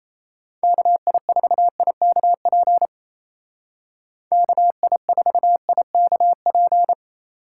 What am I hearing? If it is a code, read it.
Morse code, heard as a keyed tone, about 30 wpm, 700 Hz. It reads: KI4IKP KI4IKP
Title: Speed Racing - USA 2x3 Call Signs 28wpm